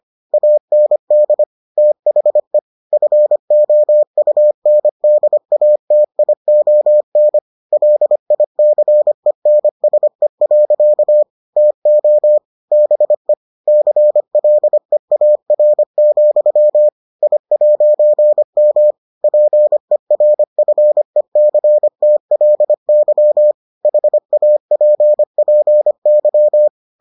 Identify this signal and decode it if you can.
Morse code, heard as a keyed tone, about 25 wpm, 600 Hz. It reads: AND THE FOUNDATION LICENSE. TO BE CLEAR, I'M PERFECTLY HAPPY